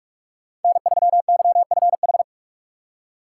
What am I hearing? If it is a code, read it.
Morse code, heard as a keyed tone, about 44 wpm, 690 Hz. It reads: N3YFH
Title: USA Call Signs - 5 Characters - 6x Character Spacing - 45wpm